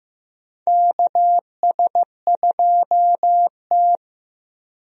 Morse code, heard as a keyed tone, about 15 wpm, 700 Hz. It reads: KS2T
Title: USA Call Signs - 4 Characters - 15wpm